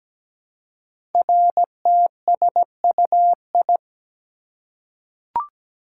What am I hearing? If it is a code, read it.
Morse code, heard as a keyed tone, about 17 wpm, 700 Hz. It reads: RTSUI
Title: Code Groups - 5 Letters - 17wpm